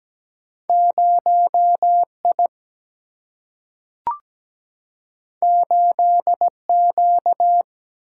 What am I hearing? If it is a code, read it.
Morse code, heard as a keyed tone, about 17 wpm, 700 Hz. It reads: 0I 8Q
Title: Single-Letter Single-Number 17wpm